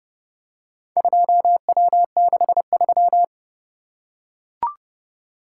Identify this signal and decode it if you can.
Morse code, heard as a keyed tone, about 29 wpm, 700 Hz. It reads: 2W63